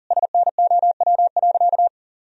Morse code, heard as a keyed tone, about 40 wpm, 700 Hz. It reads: SNOW.
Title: Sentences from Top 1000 Words - Review Part 3 - 40wpm